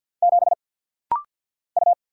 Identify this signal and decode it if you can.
Morse code, heard as a keyed tone, about 49 wpm, 700 Hz. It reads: 7 U
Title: Single Letter-Number - Rapid-Fire 50wpm